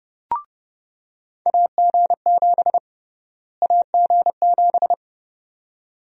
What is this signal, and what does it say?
Morse code, heard as a keyed tone, about 32 wpm, 700 Hz. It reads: AG7 AG7
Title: USA Call Sign Prefixes - Repeated 2x - 30wpm